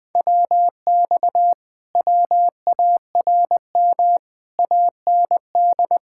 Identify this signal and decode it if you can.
Morse code, heard as a keyed tone, about 20 wpm, 700 Hz. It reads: WX WARM AND
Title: Speed-Racing Common QSO Phrases - 17wpm